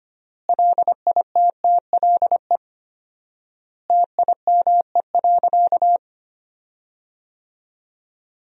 Morse code, heard as a keyed tone, about 25 wpm, 700 Hz. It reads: LITTLE TIME.
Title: 4x Word-Spacing - Sentences from Top 200 Words - 25wpm